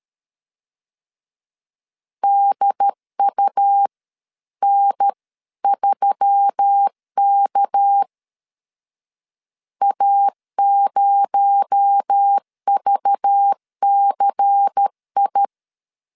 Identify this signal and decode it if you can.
Morse code, heard as a keyed tone, about 13 wpm, spaced as for 11 wpm, 800 Hz. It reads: DUN3K A0VCI
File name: CW / DET / 1 6 1500C S B 12wpm 3431